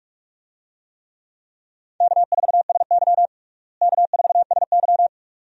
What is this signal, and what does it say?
Morse code, heard as a keyed tone, about 46 wpm, 700 Hz. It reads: K4SY K4SY
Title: USA Call Signs - 4 Characters - Repeated 2x - 45wpm